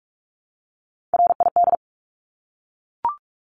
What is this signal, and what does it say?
Morse code, heard as a keyed tone, about 44 wpm, 700 Hz. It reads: RID
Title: Top 1751-2000 Words 45wpm